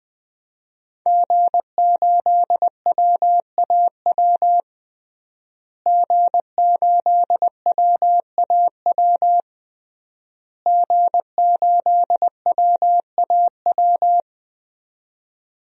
Morse code, heard as a keyed tone, about 21 wpm, 700 Hz. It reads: G8WAW G8WAW G8WAW
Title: DXCC-100 Call Signs - 5 Characters - Repeated 3x - 20wpm